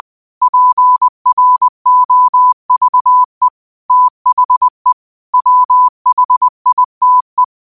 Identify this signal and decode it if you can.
Morse code, heard as a keyed tone, about 20 wpm, 1000 Hz. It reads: PROVE THE WHITE
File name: qotdfiles / 20wpm